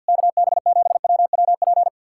Morse code, heard as a keyed tone, about 66 wpm, spaced as for 58 wpm, 700 Hz. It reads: KB7WWP
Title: Speed-Racing Common QSO Phrases - 50wpm